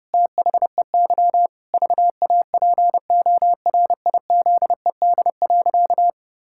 Morse code, heard as a keyed tone, about 32 wpm, 700 Hz. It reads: THEY VAPORIZED.
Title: Pangram Challenge v2 - 30wpm